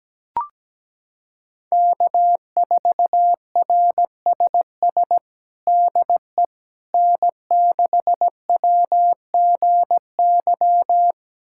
Morse code, heard as a keyed tone, about 17 wpm, 700 Hz. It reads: K4RSS DE N6WGY
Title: USA Callsign DE Callsign 17wpm